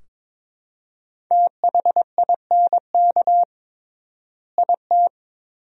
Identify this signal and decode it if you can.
Morse code, heard as a keyed tone, about 22 wpm, 700 Hz. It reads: THINK IT
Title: 3x Word-Spacing - Sentences from Top 200 Words - 22wpm